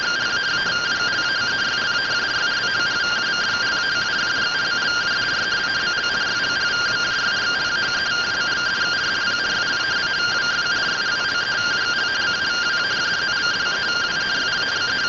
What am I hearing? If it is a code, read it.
Two-tone teleprinter signal, about 170 Hz shift.